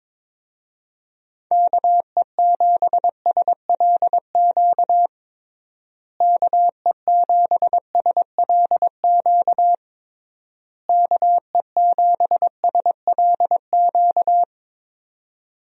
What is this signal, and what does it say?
Morse code, heard as a keyed tone, about 22 wpm, 700 Hz. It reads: KE7SLQ KE7SLQ KE7SLQ
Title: USA Call Signs - 6 Characters - Repeated 3x - 22wpm